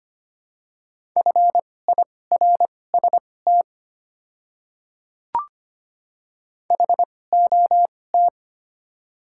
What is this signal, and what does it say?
Morse code, heard as a keyed tone, about 25 wpm, spaced as for 18 wpm, 700 Hz. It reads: FIRST HOT
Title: 2x Character Spacing - Top 100 Words - 25wpm